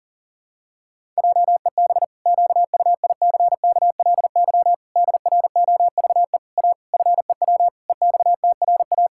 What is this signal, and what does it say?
Morse code, heard as a keyed tone, about 40 wpm, 700 Hz. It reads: JEB QUICKLY DROVE A FEW EXTRA
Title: Pangram Challenge v2 - 40wpm